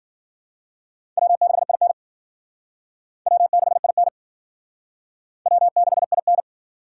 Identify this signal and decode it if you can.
Morse code, heard as a keyed tone, about 51 wpm, 700 Hz. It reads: W6IN W6IN W6IN